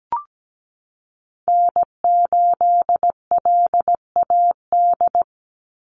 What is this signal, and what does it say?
Morse code, heard as a keyed tone, about 17 wpm, 700 Hz. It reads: N8LAD
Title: USA Call Signs 17wpm